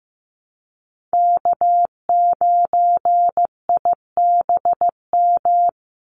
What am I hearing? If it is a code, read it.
Morse code, heard as a keyed tone, about 15 wpm, 700 Hz. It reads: K9IBM
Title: Call Signs Repeated 2x - 15wpm